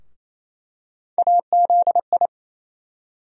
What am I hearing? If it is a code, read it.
Morse code, heard as a keyed tone, about 28 wpm, 700 Hz. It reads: AZI
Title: USA Call Sign Postfixes - 3 Letters 28wpm